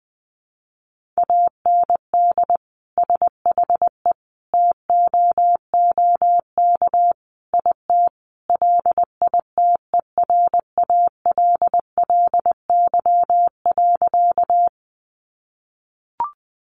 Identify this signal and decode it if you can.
Morse code, heard as a keyed tone, about 20 wpm, 700 Hz. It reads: AND SHE TOOK IT LITERALLY.
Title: Sentences from Top 1400 Words 20wpm